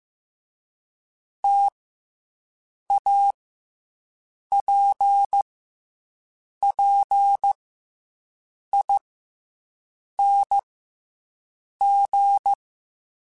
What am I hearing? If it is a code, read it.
Morse code, heard as a keyed tone, about 15 wpm, spaced as for 6 wpm, 800 Hz. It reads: TAPPING
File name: Difficult Combos / Difficult Combos-mp3s / T AOEI @13